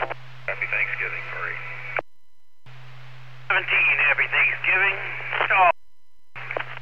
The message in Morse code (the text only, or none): none